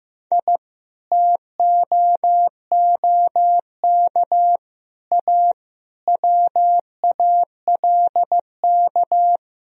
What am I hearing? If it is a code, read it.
Morse code, heard as a keyed tone, about 16 wpm, 700 Hz. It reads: I TOOK A WALK
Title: Sentences from Top 1000 Words - Review Part 4 - 15wpm